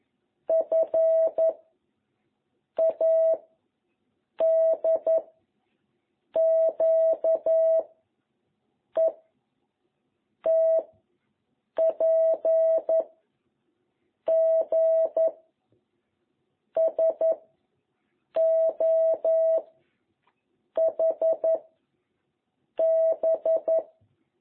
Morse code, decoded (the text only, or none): F A D Q E T P G S O H B